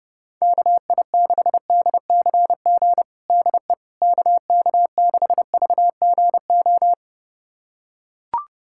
Morse code, heard as a keyed tone, about 30 wpm, 700 Hz. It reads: KI6DCG DE KK6VGO
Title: USA Callsign DE Callsign - 6 Characters - 30wpm